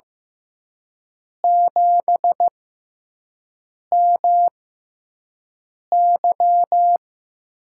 Morse code, heard as a keyed tone, about 15 wpm, spaced as for 5 wpm, 700 Hz. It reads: 7MY
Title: USA Call Signs - 6 Characters - 6x Character Spacing - 15wpm